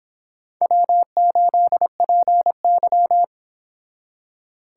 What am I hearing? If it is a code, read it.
Morse code, heard as a keyed tone, about 26 wpm, 700 Hz. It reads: W8PY